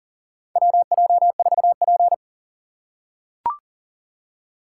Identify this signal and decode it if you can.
Morse code, heard as a keyed tone, about 40 wpm, 700 Hz. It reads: WJ4P
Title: USA Call Signs - 4 Characters - 4x Character Spacing - 40wpm